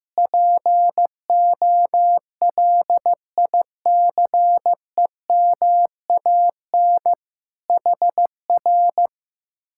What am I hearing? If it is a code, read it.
Morse code, heard as a keyed tone, about 15 wpm, 700 Hz. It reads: POLICEMAN HR
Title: Speed-Racing Common QSO Phrases Encore - 15wpm